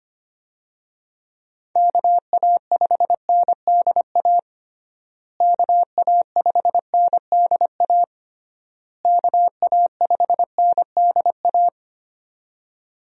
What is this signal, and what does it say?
Morse code, heard as a keyed tone, about 25 wpm, 700 Hz. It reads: KA5NDA KA5NDA KA5NDA
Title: USA Call Signs - 6 Characters - Repeated 3x - 25wpm